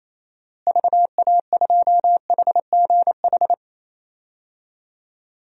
Morse code, heard as a keyed tone, about 28 wpm, 700 Hz. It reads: VA2HGH
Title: DXCC-100 Call Signs - 6 Characters - Repeated 2x - 28wpm